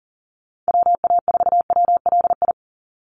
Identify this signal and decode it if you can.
Morse code, heard as a keyed tone, about 40 wpm, 700 Hz. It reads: WA4WLI
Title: Call Signs Repeated 2x - 40wpm